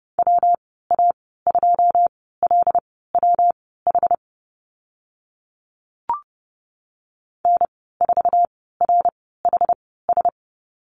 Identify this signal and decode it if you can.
Morse code, heard as a keyed tone, about 30 wpm, spaced as for 17 wpm, 700 Hz. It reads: WA2LWH N4RHS
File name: Lesson 19g - Letter w - Call Signs 30wpm